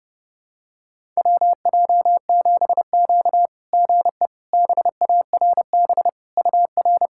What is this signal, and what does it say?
Morse code, heard as a keyed tone, about 30 wpm, 690 Hz. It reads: WJ7Q GE BARB UR
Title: SOTA QSO Practice 30wpm